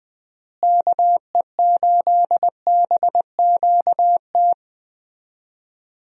Morse code, heard as a keyed tone, about 21 wpm, 700 Hz. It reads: KE8BQT
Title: USA Call Signs - 6 Characters - Repeated 2x - 20wpm